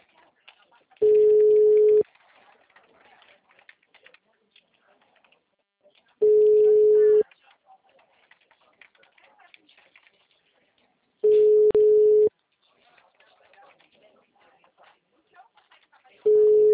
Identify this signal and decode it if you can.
Ringback tone (Europe, 425 Hz), with speech.